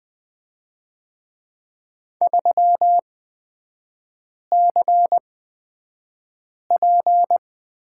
Morse code, heard as a keyed tone, about 20 wpm, spaced as for 5 wpm, 700 Hz. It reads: 3CP
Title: 255 Lesson 41e - Period - Three Characters 20wpm